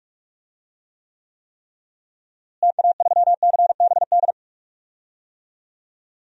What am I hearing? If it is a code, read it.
Morse code, heard as a keyed tone, about 46 wpm, 700 Hz. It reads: TA3CBD
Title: DXCC-100 Call Signs - 6 Characters - Repeated 3x - 45wpm